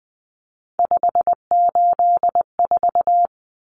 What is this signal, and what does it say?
Morse code, heard as a keyed tone, about 20 wpm, 700 Hz. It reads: 584
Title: Numbers - 3-Digit 20wpm